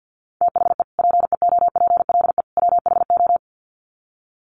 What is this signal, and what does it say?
Morse code, heard as a keyed tone, about 50 wpm, 700 Hz. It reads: THE PEOPLE WHO
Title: Sets of 3 Words 50wpm